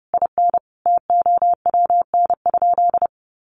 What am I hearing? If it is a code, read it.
Morse code, heard as a keyed tone, about 30 wpm, 700 Hz. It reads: IN TOWN?
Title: Sentences from Top 500 Words 30wpm